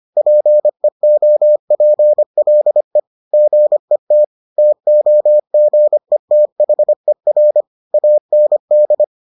Morse code, heard as a keyed tone, about 25 wpm, 600 Hz. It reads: PEOPLE GET TOGETHER AND